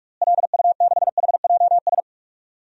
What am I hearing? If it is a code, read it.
Morse code, heard as a keyed tone, about 45 wpm, 700 Hz. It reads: LU6HJS
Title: DXCC-100 Call Signs - 6 Characters - Repeated 3x - 45wpm